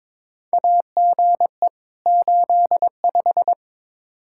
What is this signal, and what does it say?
Morse code, heard as a keyed tone, about 22 wpm, 700 Hz. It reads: AGE 85
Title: Practice QSO - Exchange 3 - More Info - 22wpm